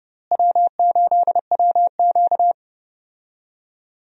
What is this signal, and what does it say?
Morse code, heard as a keyed tone, about 30 wpm, 700 Hz. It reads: W8WQ
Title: USA Call Signs - 4 Characters - Repeated 2x - 30wpm